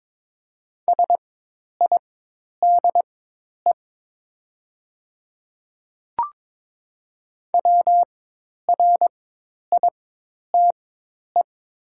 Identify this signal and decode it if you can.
Morse code, heard as a keyed tone, about 22 wpm, spaced as for 10 wpm, 700 Hz. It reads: SIDE WRITE
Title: 4x Character Spacing - Top 100 Words - 22wpm